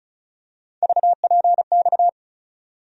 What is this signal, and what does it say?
Morse code, heard as a keyed tone, about 36 wpm, 700 Hz. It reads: VPX
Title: USA Call Sign Postfixes - 3 Letters 35wpm